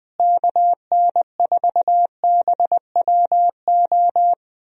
Morse code, heard as a keyed tone, about 20 wpm, 700 Hz. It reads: KN4BWO
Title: USA Call Signs - 6 Characters - Repeated 2x - 20wpm